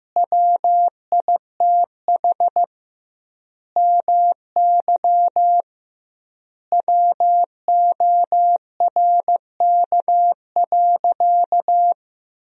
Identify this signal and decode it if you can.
Morse code, heard as a keyed tone, about 16 wpm, 700 Hz. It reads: WITH MY WORK.
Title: 2x Word-Spacing - Sentences from Top 300 Words - 15wpm